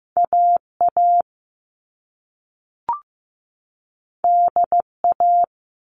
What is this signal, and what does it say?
Morse code, heard as a keyed tone, about 15 wpm, 700 Hz. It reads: AA DA
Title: All Cut Numbers - 2-Digit 15wpm